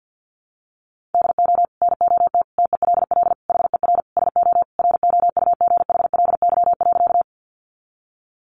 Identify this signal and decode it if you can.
Morse code, heard as a keyed tone, about 50 wpm, 700 Hz. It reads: DO NOT TELL HER SO ROUGHLY.
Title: Sentences from Top 1500 Words 50wpm